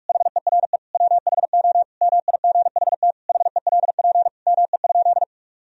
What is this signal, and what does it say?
Morse code, heard as a keyed tone, about 46 wpm, 700 Hz. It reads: HERE WHO MIGHT HELP ME?
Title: Sentences from Top 1000 Words - Review Part 3 - 45wpm